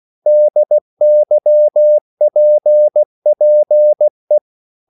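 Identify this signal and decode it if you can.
Morse code, heard as a keyed tone, about 16 wpm, 600 Hz. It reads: DYPPE
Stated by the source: Slow (QRS) Morse code at 16 wpm, from CW med Gnister.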